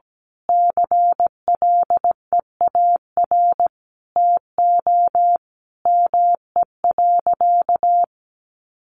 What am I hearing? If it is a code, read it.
Morse code, heard as a keyed tone, about 17 wpm, 700 Hz. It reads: CLEAR TO ME.
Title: Sentences from Top 1300 Words 17wpm